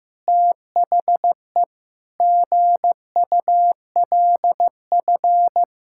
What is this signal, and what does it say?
Morse code, heard as a keyed tone, about 16 wpm, 700 Hz. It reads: THE GULF
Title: Pangram Challenge v2 - 15wpm